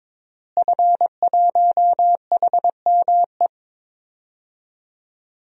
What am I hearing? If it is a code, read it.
Morse code, heard as a keyed tone, about 22 wpm, 700 Hz. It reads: F1HME
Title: DXCC-100 Call Signs - 5 Characters - Repeated 3x - 22wpm